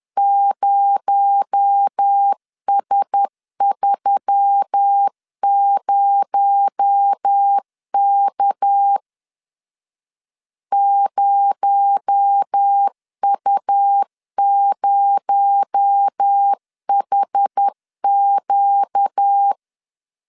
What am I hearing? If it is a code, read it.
Morse code, heard as a keyed tone, about 10 wpm, 800 Hz. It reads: 0S30K 0U0HQ
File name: LEZIONI / 6 PKX0 / 6 PKX0 10wpm 2707